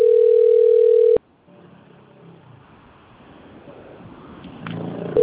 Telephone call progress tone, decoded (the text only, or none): ringback tone